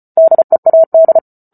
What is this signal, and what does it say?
Morse code, heard as a keyed tone, about 34 wpm, 650 Hz. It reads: DEAD